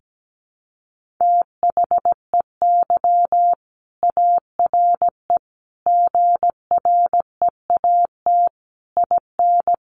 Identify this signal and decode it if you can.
Morse code, heard as a keyed tone, about 17 wpm, 700 Hz. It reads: THEY ARE GREAT IN
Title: Sentences from Top 1400 Words 17wpm